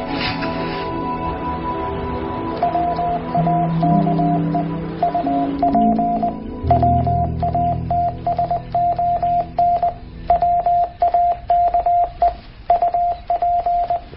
Morse code, heard as a keyed tone, about 20 wpm, 700 Hz, with radio noise. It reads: WAKE UP WATSON WAKE UP